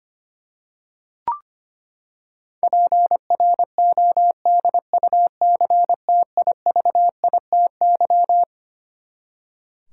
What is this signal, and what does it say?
Morse code, heard as a keyed tone, about 25 wpm, 700 Hz. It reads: PRODUCTIVITY
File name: Twelve Letter Words - 25wpm